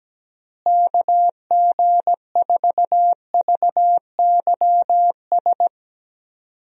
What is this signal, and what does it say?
Morse code, heard as a keyed tone, about 17 wpm, 700 Hz. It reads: KG4VYS
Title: USA Call Signs - 6 Characters - Repeated 2x - 17wpm